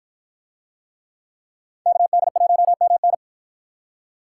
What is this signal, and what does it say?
Morse code, heard as a keyed tone, about 53 wpm, 700 Hz. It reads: KD1MN